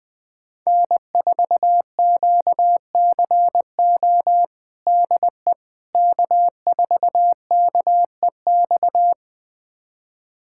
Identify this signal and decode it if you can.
Morse code, heard as a keyed tone, about 20 wpm, 700 Hz. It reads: N4QCO DE K4KEX
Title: USA Callsign DE Callsign 20wpm